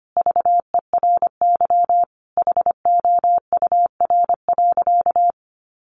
Morse code, heard as a keyed tone, about 25 wpm, 700 Hz. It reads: VERY HOUR.
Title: Sentences from Top 1500 Words 25wpm